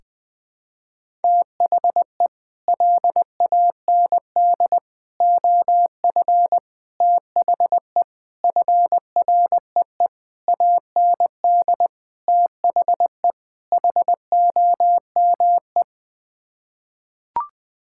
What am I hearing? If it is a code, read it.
Morse code, heard as a keyed tone, about 20 wpm, 700 Hz. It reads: THE LAND OF THE FREE AND THE HOME
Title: Sets of 8 Words - 20wpm